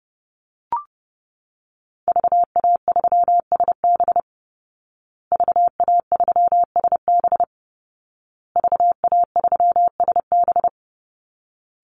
Morse code, heard as a keyed tone, about 30 wpm, 700 Hz. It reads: VA3SB VA3SB VA3SB
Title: Call Signs Repeated 2x - 30wpm